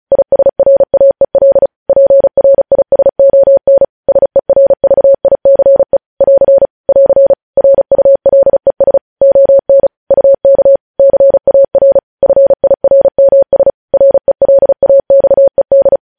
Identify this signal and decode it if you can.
Morse code, heard as a keyed tone, about 36 wpm, 550 Hz. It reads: ISRAEL PRISON SERVICE <AR> <AR> RULES ON UK CAR FIRMS RELAXED
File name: podcast / News headlines-1x-35x35